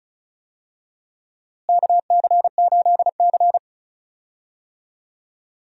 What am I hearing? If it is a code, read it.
Morse code, heard as a keyed tone, about 36 wpm, 700 Hz. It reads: KC8C
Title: USA Call Signs - 4 Characters - Repeated 3x - 35wpm